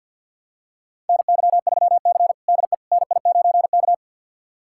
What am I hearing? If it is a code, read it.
Morse code, heard as a keyed tone, about 52 wpm, 700 Hz. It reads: NY3C DE NI9K
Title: USA Callsign DE Callsign - 4 Characters - 50wpm